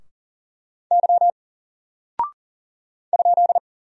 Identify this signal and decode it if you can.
Morse code, heard as a keyed tone, about 39 wpm, 700 Hz. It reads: Y ?
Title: Single Character (Letter:Number:Punctuation:Prosign) Alt 40wpm